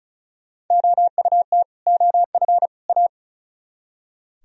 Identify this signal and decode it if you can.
Morse code, heard as a keyed tone, about 45 wpm, 700 Hz. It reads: OUT OF A